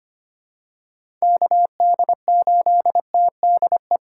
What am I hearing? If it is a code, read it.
Morse code, heard as a keyed tone, about 25 wpm, 700 Hz. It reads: KD8TDE
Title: USA Call Signs - 6 Characters - 3x Character Spacing - 25wpm